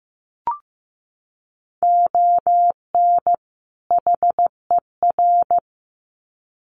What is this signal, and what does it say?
Morse code, heard as a keyed tone, about 15 wpm, 700 Hz. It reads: ON HER
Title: Sets of 2 Words 15wpm